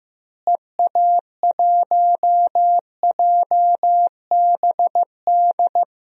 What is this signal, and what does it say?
Morse code, heard as a keyed tone, about 16 wpm, 700 Hz. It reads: EA1JBD
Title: DXCC-100 Call Signs - 6 Characters - Repeated 2x - 15wpm